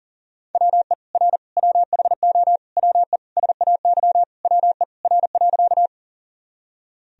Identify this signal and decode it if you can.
Morse code, heard as a keyed tone, about 40 wpm, 700 Hz. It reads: WE R WHO WE SAY WE R.
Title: Sentences from Top 200 Words and Abbreviations 40wpm